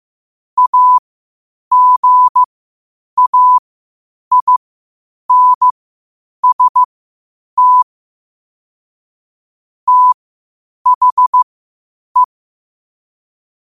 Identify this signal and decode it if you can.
Morse code, heard as a keyed tone, about 15 wpm, spaced as for 8 wpm, 1000 Hz. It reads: AGAINST THE
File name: 8wpm